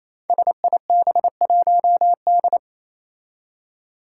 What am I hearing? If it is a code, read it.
Morse code, heard as a keyed tone, about 28 wpm, 700 Hz. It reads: SIB1D
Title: Five Challenging Letter-Number Combinations 28wpm